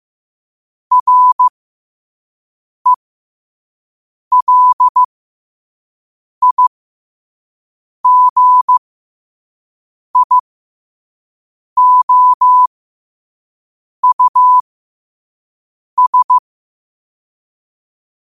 Morse code, heard as a keyed tone, about 15 wpm, spaced as for 5 wpm, 1000 Hz. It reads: RELIGIOUS
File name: qotdfiles / 5wpm